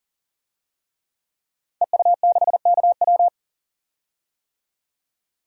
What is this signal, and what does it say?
Morse code, heard as a keyed tone, about 40 wpm, 700 Hz. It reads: EU6KW
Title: DXCC-100 Call Signs - 5 Characters - 40wpm